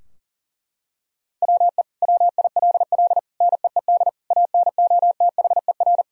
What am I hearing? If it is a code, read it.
Morse code, heard as a keyed tone, about 40 wpm, 700 Hz. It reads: WE WILL NEED ANOTHER
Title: Sentences from Top 1000 Words - Review Part 4 - 40wpm